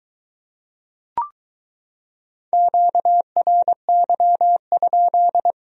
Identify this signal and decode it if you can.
Morse code, heard as a keyed tone, about 22 wpm, 700 Hz. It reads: QRY?